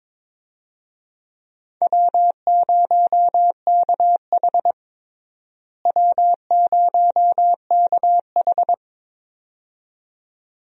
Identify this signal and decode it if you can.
Morse code, heard as a keyed tone, about 22 wpm, 700 Hz. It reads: W0KH W0KH
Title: USA Call Signs - 4 Characters - Repeated 2x - 22wpm